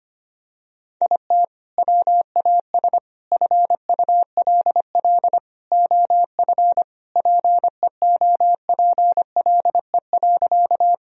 Morse code, heard as a keyed tone, about 25 wpm, 700 Hz. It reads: IT WAS FULL OF PEOPLE.
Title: Sentences from Top 1000 Words - Review Part 3 - 25wpm